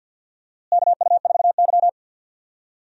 Morse code, heard as a keyed tone, about 49 wpm, 700 Hz. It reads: KU4Y
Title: USA Call Signs - 4 Characters - Repeated 2x - 50wpm